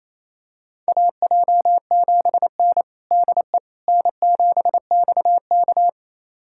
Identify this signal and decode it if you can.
Morse code, heard as a keyed tone, about 28 wpm, 700 Hz. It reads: AJ7N DE N7XK
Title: USA Callsign DE Callsign - 4 Characters - 28wpm